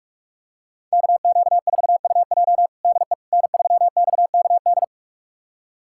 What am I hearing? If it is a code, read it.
Morse code, heard as a keyed tone, about 46 wpm, 700 Hz. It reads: KQ4UJ DE N3XKD
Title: USA Callsign DE Callsign - 5 Characters - 45wpm